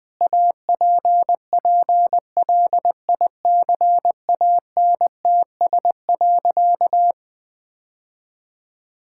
Morse code, heard as a keyed tone, about 20 wpm, 700 Hz. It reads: APPLICANTS.
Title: Sentences with 6-Word Phrases - Spoken Beginning - 20wpm